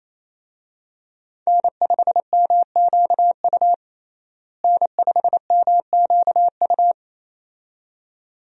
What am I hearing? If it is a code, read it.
Morse code, heard as a keyed tone, about 28 wpm, 700 Hz. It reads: N5MQU N5MQU
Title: USA Call Signs - 5 Characters - Repeated 2x - 28wpm